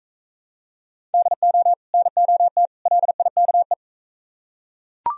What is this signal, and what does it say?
Morse code, heard as a keyed tone, about 40 wpm, 700 Hz. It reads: DO NOT LIKE